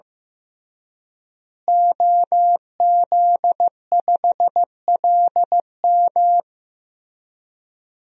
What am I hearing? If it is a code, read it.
Morse code, heard as a keyed tone, about 15 wpm, 700 Hz. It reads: OZ5LM
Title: DXCC-100 Call Signs - 5 Characters - Repeated 3x - 15wpm